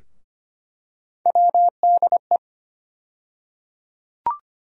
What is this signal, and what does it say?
Morse code, heard as a keyed tone, about 25 wpm, 700 Hz. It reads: WDE
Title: USA Call Sign Postfixes - 3 Letters 25wpm